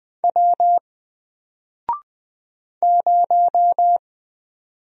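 Morse code, heard as a keyed tone, about 20 wpm, 700 Hz. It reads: W 0
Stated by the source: Single letter and number Morse drill at 20 wpm, from N4LCD.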